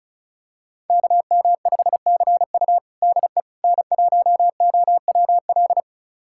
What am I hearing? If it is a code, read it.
Morse code, heard as a keyed tone, about 36 wpm, 700 Hz. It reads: KM5CU DE N1OWL
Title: USA Callsign DE Callsign - 5 Characters - 35wpm